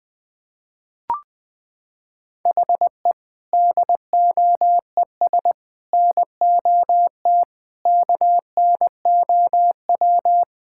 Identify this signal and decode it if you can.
Morse code, heard as a keyed tone, about 21 wpm, 700 Hz. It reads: HE DOES NOT KNOW
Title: Sentences with 4-Word Phrases - 20wpm